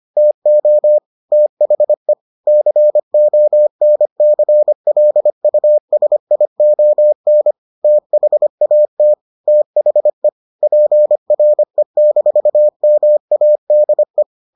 Morse code, heard as a keyed tone, about 25 wpm, 600 Hz. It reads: TO THE CONCLUSION THAT THE PRE-MADE